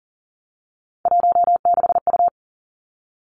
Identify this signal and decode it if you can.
Morse code, heard as a keyed tone, about 38 wpm, 690 Hz. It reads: 16U
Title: Three Challenging Letter-Number Combinations 40wpm